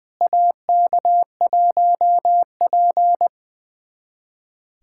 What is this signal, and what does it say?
Morse code, heard as a keyed tone, about 20 wpm, 700 Hz. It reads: AK1P
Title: USA Callsign DE Callsign - 4 Characters - 20wpm